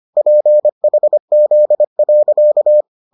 Morse code, heard as a keyed tone, about 25 wpm, 600 Hz. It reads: PHZ.